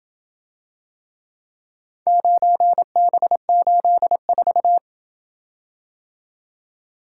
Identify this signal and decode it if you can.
Morse code, heard as a keyed tone, about 28 wpm, 700 Hz. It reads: 9B84